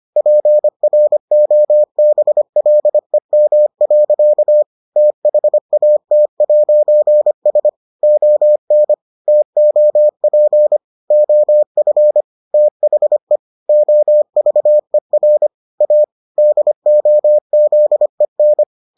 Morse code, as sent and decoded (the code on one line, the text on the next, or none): .--. .-. --- -... .-.. -- .-.-.- / - .... .- - .----. ... / --- -. / - --- .--. / --- ..-. / - .... / --- ...- . .-. / .- / -.. --- --.. -.
PROBLEM. THAT'S ON TOP OF THE OVER A DOZEN